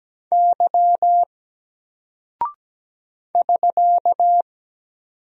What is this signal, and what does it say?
Morse code, heard as a keyed tone, about 17 wpm, 700 Hz. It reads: Y <SK>
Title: Single Character (Letter:Number:Punctuation:Prosign) 17wpm